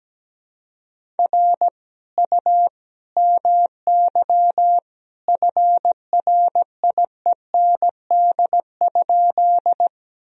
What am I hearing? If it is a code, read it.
Morse code, heard as a keyed tone, about 17 wpm, 700 Hz. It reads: R U MY FRIEND?
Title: Sentences from Top 300 Words and Abbreviations 17wpm